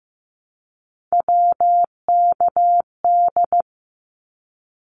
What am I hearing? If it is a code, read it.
Morse code, heard as a keyed tone, about 14 wpm, 700 Hz. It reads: WKD
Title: Abbreviations Repeated 3x - 15wpm